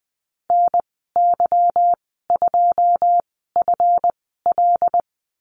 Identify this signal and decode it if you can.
Morse code, heard as a keyed tone, about 20 wpm, spaced as for 15 wpm, 700 Hz. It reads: NY2FL
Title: 173 Lesson 28g - Number 9 - Call Signs 20wpm